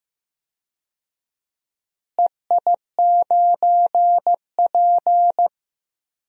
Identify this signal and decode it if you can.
Morse code, heard as a keyed tone, about 15 wpm, 700 Hz. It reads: EI9P
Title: DXCC-100 Call Signs - 4 Characters - Repeated 3x - 15wpm